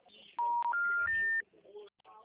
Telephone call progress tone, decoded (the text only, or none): special information tone